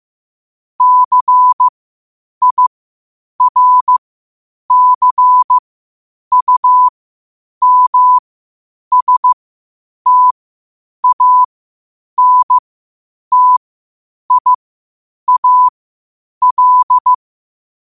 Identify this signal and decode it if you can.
Morse code, heard as a keyed tone, about 15 wpm, spaced as for 8 wpm, 1000 Hz. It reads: CIRCUMSTANTIAL
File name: qotdfiles / 8wpm